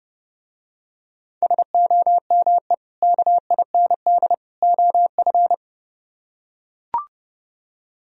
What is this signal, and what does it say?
Morse code, heard as a keyed tone, about 31 wpm, 700 Hz. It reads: SOME KIND OF